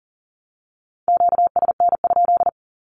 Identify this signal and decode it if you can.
Morse code, heard as a keyed tone, about 38 wpm, 700 Hz. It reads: QSN?
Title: Q-Codes with Spoken Meaning - 40wpm